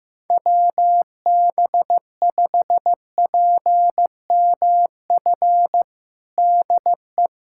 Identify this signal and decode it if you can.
Morse code, heard as a keyed tone, about 15 wpm, 700 Hz. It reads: WB5PMF DE
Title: USA Callsign DE Callsign 15wpm